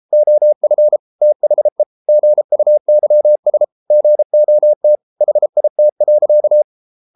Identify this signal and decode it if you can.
Morse code, heard as a keyed tone, about 33 wpm, 600 Hz. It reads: OF THE GUYS GOT HIT.